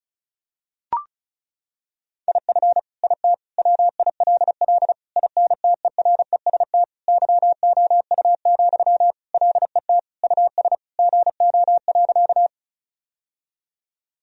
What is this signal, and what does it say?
Morse code, heard as a keyed tone, about 36 wpm, 700 Hz. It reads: IF IT WILL INTEREST YOU, LET US GO.
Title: Sentences from Top 1000 Words - Review Part 2 - 35wpm